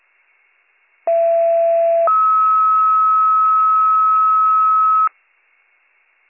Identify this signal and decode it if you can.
Two-tone page, heard as a one-second tone, then a three-second tone, about 670 then 1290 Hz. It is heard from 1.1 to 5.1 s.